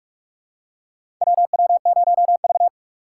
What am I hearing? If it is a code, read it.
Morse code, heard as a keyed tone, about 46 wpm, 700 Hz. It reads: WW0V
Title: USA Call Signs - 3x Character Spacing - 45wpm